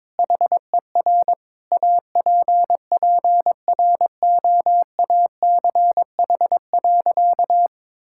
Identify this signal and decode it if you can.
Morse code, heard as a keyed tone, about 22 wpm, 700 Hz. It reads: HER APPROACH.
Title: Sentences from Top 1000 Words - Review Part 6 - 22wpm